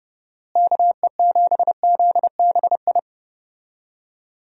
Morse code, heard as a keyed tone, about 30 wpm, 700 Hz. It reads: KE7ZBI
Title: USA Call Signs - 6 Characters - Repeated 2x - 30wpm